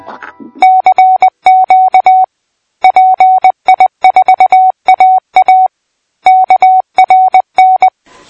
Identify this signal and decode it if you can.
Morse code, heard as a keyed tone, about 20 wpm, 750 Hz. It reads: CQ PI4AA KRN
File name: pi4aa / PI4AA Uitzending